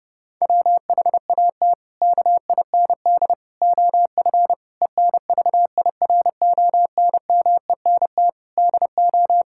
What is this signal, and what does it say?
Morse code, heard as a keyed tone, about 30 wpm, 700 Hz. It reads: WHAT KIND OF ENVIRONMENT DO
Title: Sentences from Top 1000 Words - Review Part 5 - 30wpm